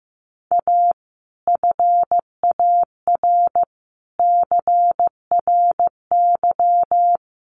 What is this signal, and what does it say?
Morse code, heard as a keyed tone, about 15 wpm, 700 Hz. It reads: A FAR CRY
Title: Sets of 5 Words Encore 15wpm